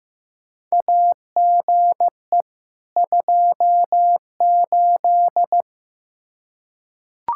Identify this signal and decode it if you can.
Morse code, heard as a keyed tone, about 15 wpm, 700 Hz. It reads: AGE 28
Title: K1USN SST QSO Practice 15wpm